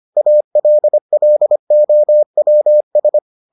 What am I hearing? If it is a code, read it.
Morse code, heard as a keyed tone, about 25 wpm, 600 Hz. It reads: ALLOWS